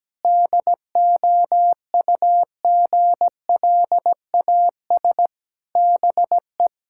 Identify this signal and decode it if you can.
Morse code, heard as a keyed tone, about 17 wpm, 700 Hz. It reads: DOUGLAS BE
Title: Pangram Challenge v2 - 17wpm